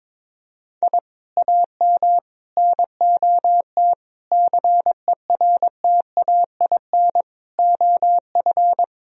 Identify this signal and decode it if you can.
Morse code, heard as a keyed tone, about 22 wpm, 700 Hz. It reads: I AM NOT CERTAIN OF
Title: Sentences from Top 1000 Words - Review Part 2 - 22wpm